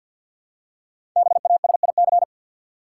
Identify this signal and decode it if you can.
Morse code, heard as a keyed tone, about 47 wpm, 700 Hz. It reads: BASIC
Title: Five Letter Words - 50wpm